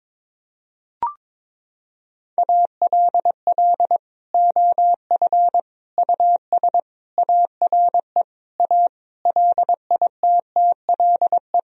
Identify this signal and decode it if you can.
Morse code, heard as a keyed tone, about 22 wpm, 700 Hz. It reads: ALL OF US ARE A LITTLE
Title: Sentences from Top 1000 Words - Review Part 3 - 22wpm